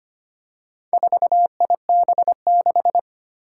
Morse code, heard as a keyed tone, about 25 wpm, 700 Hz. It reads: 4IB6
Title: Four Challenging Letter-Number Combinations 25wpm